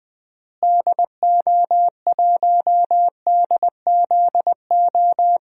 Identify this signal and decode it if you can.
Morse code, heard as a keyed tone, about 20 wpm, 700 Hz. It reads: DO1DZO
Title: DXCC-100 Call Signs - 6 Characters - Repeated 2x - 20wpm